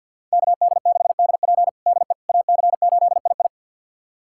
Morse code, heard as a keyed tone, about 50 wpm, 700 Hz. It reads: KD6DP DE AC8II
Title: USA Callsign DE Callsign - 5 Characters - 50wpm